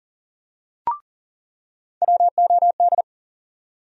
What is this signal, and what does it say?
Morse code, heard as a keyed tone, about 40 wpm, 700 Hz. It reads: WOD
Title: USA Call Sign Postfixes - 3 Letters 40wpm